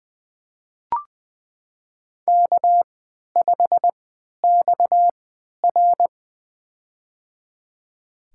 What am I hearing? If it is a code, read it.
Morse code, heard as a keyed tone, about 21 wpm, spaced as for 11 wpm, 700 Hz. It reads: K5XR
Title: USA Call Signs - 4 Characters - 3x Character Spacing - 20wpm